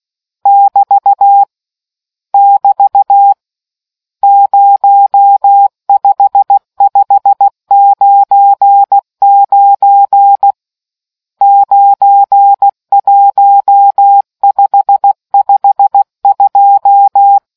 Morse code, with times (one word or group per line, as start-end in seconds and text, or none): 0.5-1.4 <BT>
2.3-3.3 <BT>
4.2-10.5 05599
11.4-17.4 91552